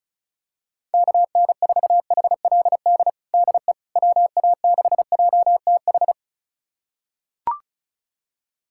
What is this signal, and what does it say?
Morse code, heard as a keyed tone, about 36 wpm, 690 Hz. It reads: KN4HLD DE WA6JTH
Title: USA Callsign DE Callsign 35wpm